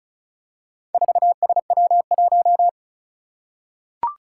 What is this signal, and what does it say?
Morse code, heard as a keyed tone, about 36 wpm, 700 Hz. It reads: 4SW1
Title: Four Challenging Letter-Number Combinations 35wpm